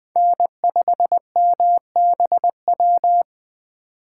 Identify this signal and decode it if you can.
Morse code, heard as a keyed tone, about 20 wpm, 700 Hz. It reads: N5MBW
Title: USA Call Signs - 5 Characters - 3x Character Spacing - 20wpm